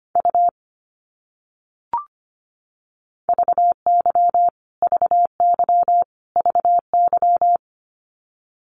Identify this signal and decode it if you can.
Morse code, heard as a keyed tone, about 25 wpm, 700 Hz. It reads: U VY VY VY
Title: Abbreviations Repeated 3x - 25wpm